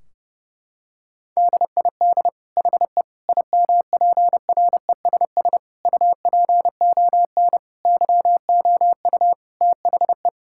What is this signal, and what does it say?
Morse code, heard as a keyed tone, about 30 wpm, 700 Hz. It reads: DID HE IMPRESS UPON YOU THE
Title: Sentences from Top 1600 Words 30wpm